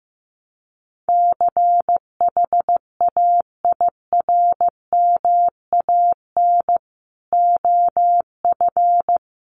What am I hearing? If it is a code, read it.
Morse code, heard as a keyed tone, about 15 wpm, 700 Hz. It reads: CHAIRMAN OF
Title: Sets of 5 Words Encore 15wpm